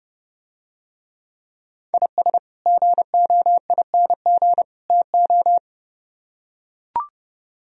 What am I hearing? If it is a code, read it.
Morse code, heard as a keyed tone, about 31 wpm, 700 Hz. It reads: IS GOING TO